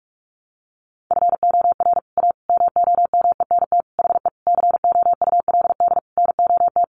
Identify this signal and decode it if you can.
Morse code, heard as a keyed tone, about 44 wpm, 700 Hz. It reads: FOR A MOMENT HE COULD NOT
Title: Sentences from Top 1100 Words 45wpm